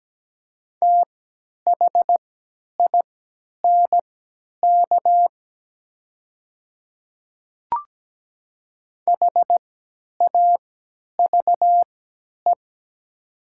Morse code, heard as a keyed tone, about 17 wpm, spaced as for 10 wpm, 700 Hz. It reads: THINK HAVE
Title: 3x Character Spacing - Top 300 Words - 17wpm